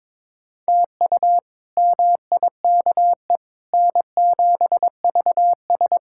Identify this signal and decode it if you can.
Morse code, heard as a keyed tone, about 22 wpm, 700 Hz. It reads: TU MIKE N7VS
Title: K1USN SST QSO Practice - 22wpm